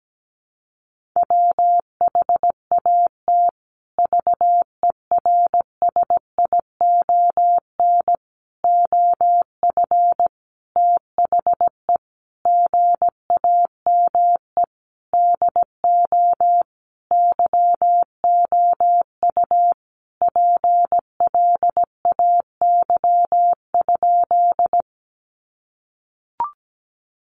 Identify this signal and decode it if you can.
Morse code, heard as a keyed tone, about 17 wpm, 700 Hz. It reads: WHAT VERSION OF THE GAME DO YOU PLAY?
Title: Sentences from Top 1400 Words 17wpm